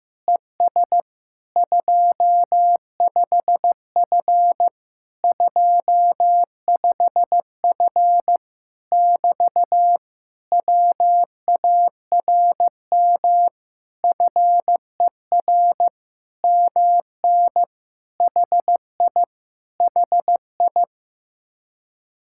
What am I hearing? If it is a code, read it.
Morse code, heard as a keyed tone, about 15 wpm, 700 Hz. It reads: ES 25F 25F <BT> WARM FER MN HI HI
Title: Speed-Racing Common QSO Phrases Encore - 15wpm